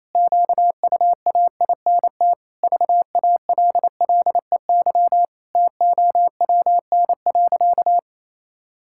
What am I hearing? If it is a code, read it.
Morse code, heard as a keyed tone, about 28 wpm, 700 Hz. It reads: QUAINT VALLEY TOWN.
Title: Pangram Challenge v2 - 28wpm